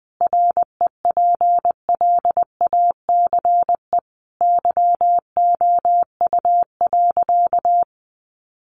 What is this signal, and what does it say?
Morse code, heard as a keyed tone, about 20 wpm, 700 Hz. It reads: REPLACE YOU.
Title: Sentences from Top 1400 Words 20wpm